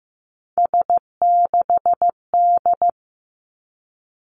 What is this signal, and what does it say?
Morse code, heard as a keyed tone, about 15 wpm, 700 Hz. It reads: S6D
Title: Three Challenging Letter-Number Combinations 15wpm